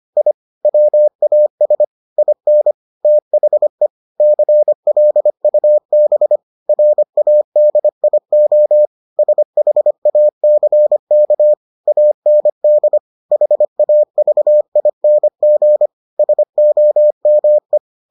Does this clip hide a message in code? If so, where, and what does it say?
Yes: Morse code from 0.2 to 17.8 s, heard as a keyed tone, about 25 wpm, 600 Hz. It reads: I WAS IN THE CLUB RADIO SHACK AND HAVING SOME